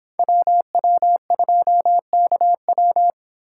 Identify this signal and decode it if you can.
Morse code, heard as a keyed tone, about 27 wpm, 700 Hz. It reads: WW2KW
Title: Speed Racing - USA 2x2 Call Signs 22wpm